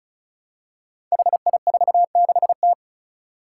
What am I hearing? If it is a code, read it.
Morse code, heard as a keyed tone, about 36 wpm, 700 Hz. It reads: HI46T
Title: Five Challenging Letter-Number Combinations 35wpm